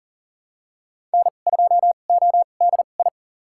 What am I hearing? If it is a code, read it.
Morse code, heard as a keyed tone, about 40 wpm, spaced as for 29 wpm, 700 Hz. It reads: N2ODI
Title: USA Call Signs - 5 Characters - 2x Character Spacing - 40wpm